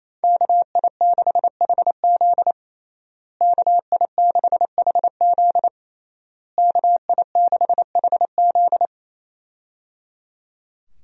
Morse code, heard as a keyed tone, about 27 wpm, 700 Hz. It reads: KI6HZ KI6HZ KI6HZ
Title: USA Call Signs - 5 Characters - Repeated 3x - 28wpm